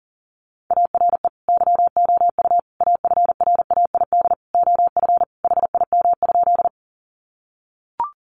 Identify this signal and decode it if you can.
Morse code, heard as a keyed tone, about 40 wpm, 700 Hz. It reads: ARE YOU AFRAID OF HIM?
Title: Sentences from Top 1500 Words 40wpm